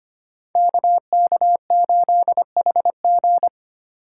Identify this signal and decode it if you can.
Morse code, heard as a keyed tone, about 25 wpm, 690 Hz. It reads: KK8HG
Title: Code Groups - 5 Letter:Numbers - 25wpm